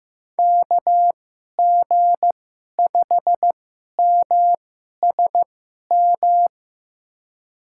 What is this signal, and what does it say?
Morse code, heard as a keyed tone, about 15 wpm, spaced as for 11 wpm, 700 Hz. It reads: KG5MSM
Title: USA Call Signs - 2x Character Spacing - 15wpm